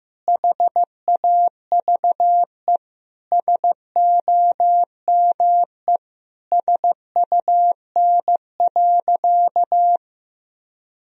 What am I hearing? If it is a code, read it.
Morse code, heard as a keyed tone, about 15 wpm, 700 Hz. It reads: HAVE SOME SUN.
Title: Sentences from Top 1000 Words - Review Part 1 - 15wpm